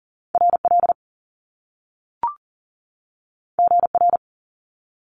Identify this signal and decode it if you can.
Morse code, heard as a keyed tone, about 38 wpm, 700 Hz. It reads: RL GR
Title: Top Two Letter Combinations in Words 40wpm